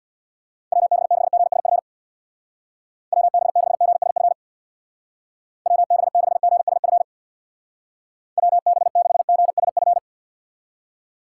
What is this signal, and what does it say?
Morse code, heard as a keyed tone, about 62 wpm, 700 Hz. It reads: WB6GSF WB6GSF WB6GSF WB6GSF
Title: Speed Racing - USA Call Signs 50wpm